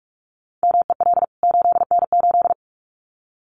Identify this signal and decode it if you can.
Morse code, heard as a keyed tone, about 44 wpm, 700 Hz. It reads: MEL 8N8
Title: CWT Exchange 45wpm